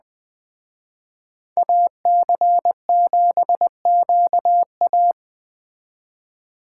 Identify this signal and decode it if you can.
Morse code, heard as a keyed tone, about 20 wpm, 700 Hz. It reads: AC7QA
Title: Speed Racing - USA 2x2 Call Signs 17wpm